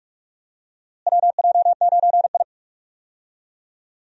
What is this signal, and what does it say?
Morse code, heard as a keyed tone, about 42 wpm, 700 Hz. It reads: WJ9I
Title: USA Call Signs - 4 Characters - Repeated 3x - 45wpm